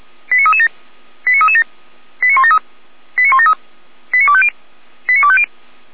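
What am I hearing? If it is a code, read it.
Five-tone selective call (ZVEI), heard as quick runs of five tones, tones of about 70 ms. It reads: 89337 89337 89173 89173 89250 89250
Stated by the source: ZVEI five-tone call received by SDR, archived by Feuerwehr Orlamünde.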